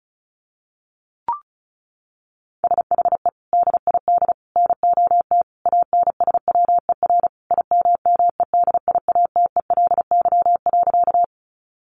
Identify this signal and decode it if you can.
Morse code, heard as a keyed tone, about 34 wpm, 700 Hz. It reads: SHE DID NOT ANSWER IMMEDIATELY.
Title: Sentences from Top 1100 Words 35wpm